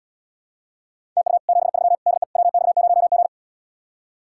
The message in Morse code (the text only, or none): TU 73 DE WX0G